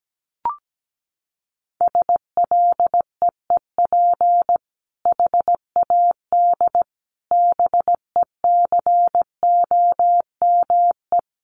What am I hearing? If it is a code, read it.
Morse code, heard as a keyed tone, about 17 wpm, 700 Hz. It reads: SLEEP HAD BECOME
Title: Sentences from Top 1100 Words 17wpm